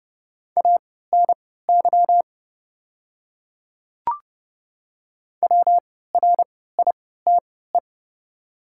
Morse code, heard as a keyed tone, about 29 wpm, spaced as for 17 wpm, 690 Hz. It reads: ANY WRITE
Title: 3x Character Spacing - Top 300 Words - 30wpm